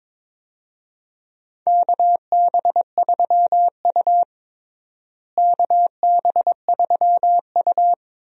Morse code, heard as a keyed tone, about 22 wpm, 700 Hz. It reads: KB3U KB3U
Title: USA Call Signs - 4 Characters - Repeated 2x - 22wpm